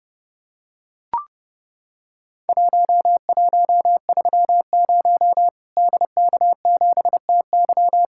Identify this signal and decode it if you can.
Morse code, heard as a keyed tone, about 30 wpm, 700 Hz. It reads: 1130 DK7TY
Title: Worked All Europe DX Contest - QTC Exchange - 30wpm